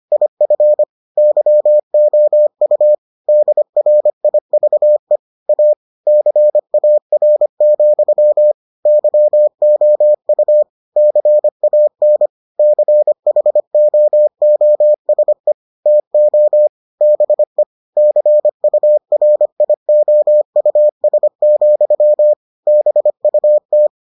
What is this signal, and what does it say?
Morse code, heard as a keyed tone, about 25 wpm, 600 Hz. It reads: IF YOU DRIVE A CAR, YOU CAN CHOOSE TO BE CURIOUS, BUT